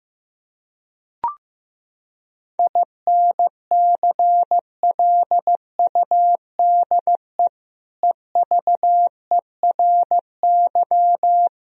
Morse code, heard as a keyed tone, about 15 wpm, 700 Hz. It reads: INCLUDE EVERY
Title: Sentences from Top 1000 Words - Review Part 4 - 15wpm